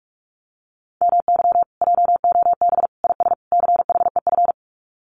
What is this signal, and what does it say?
Morse code, heard as a keyed tone, about 44 wpm, 700 Hz. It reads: MY JOB IS CHEF
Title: Common QSO Phrases Encore 45wpm